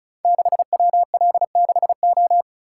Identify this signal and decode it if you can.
Morse code, heard as a keyed tone, about 36 wpm, 700 Hz. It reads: 6WL6O
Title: Code Groups - 5 Letter:Numbers - 35wpm